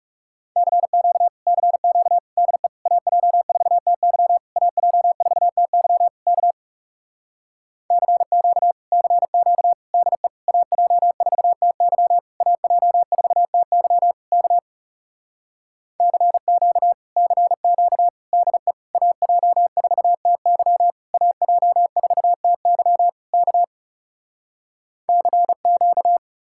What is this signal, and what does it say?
Morse code, heard as a keyed tone, about 40 wpm, 700 Hz. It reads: CQ CQ DE AJ4TY AJ4TY K CQ CQ DE AJ4TY AJ4TY K CQ CQ DE AJ4TY AJ4TY K CQ